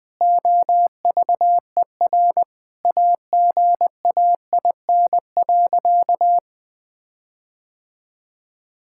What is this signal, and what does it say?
Morse code, heard as a keyed tone, about 20 wpm, 700 Hz. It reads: OVER AGAIN.
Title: Sentences with 4-Word Phrases - 20wpm